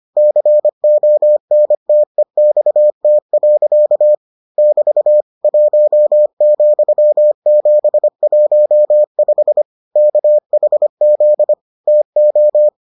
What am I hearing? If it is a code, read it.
Morse code, heard as a keyed tone, about 25 wpm, 600 Hz. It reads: CONTEXT. <BT> 1,715 KHZ TO